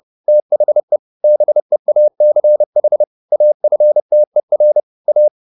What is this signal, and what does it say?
Morse code, heard as a keyed tone, about 30 wpm, 600 Hz. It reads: THE BEACH AFTER A
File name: the old man and the sea / the old man and the sea 1 30wpm